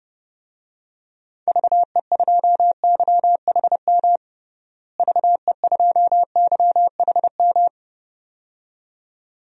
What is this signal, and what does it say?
Morse code, heard as a keyed tone, about 30 wpm, 700 Hz. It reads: VE2YHM VE2YHM
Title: DXCC-100 Call Signs - 6 Characters - Repeated 2x - 30wpm